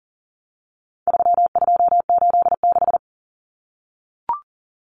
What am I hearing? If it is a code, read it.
Morse code, heard as a keyed tone, about 40 wpm, 700 Hz. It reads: 3286
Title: Common Cut Numbers - 4-Digit 40wpm